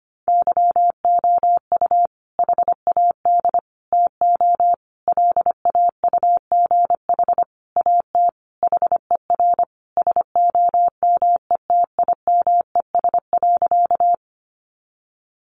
Morse code, heard as a keyed tone, about 25 wpm, 700 Hz. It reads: YOU HAD TO LAUGH AT HER SOMETIMES.
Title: Sentences from Top 600 Words 25wpm